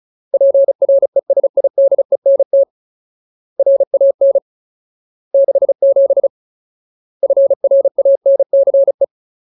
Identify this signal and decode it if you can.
Morse code, heard as a keyed tone, about 34 wpm, 550 Hz. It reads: PRESIDENT RAN 67 FRANCE
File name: Ultimate Word List / 35WPM Ultimate